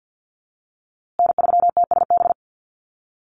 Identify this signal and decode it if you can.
Morse code, heard as a keyed tone, about 50 wpm, 700 Hz. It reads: N3TSB
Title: USA Call Signs Repeated 2x 50wpm